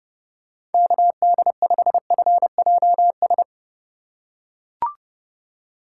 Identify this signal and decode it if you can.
Morse code, heard as a keyed tone, about 30 wpm, 700 Hz. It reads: KD5FJS